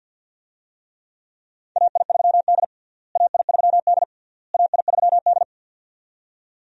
Morse code, heard as a keyed tone, about 52 wpm, 700 Hz. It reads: AI3D AI3D AI3D
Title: USA Call Signs - 4 Characters - Repeated 3x - 50wpm